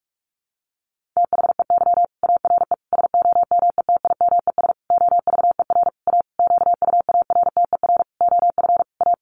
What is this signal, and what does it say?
Morse code, heard as a keyed tone, about 44 wpm, 700 Hz. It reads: THEY ARE SOMETIMES OVER A QUARTER OF A
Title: Sentences from Top 1300 Words 45wpm